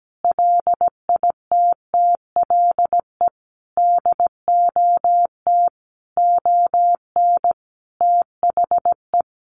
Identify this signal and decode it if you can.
Morse code, heard as a keyed tone, about 17 wpm, 700 Hz. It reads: LITTLE DOT ON THE
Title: Sentences from Top 1500 Words 17wpm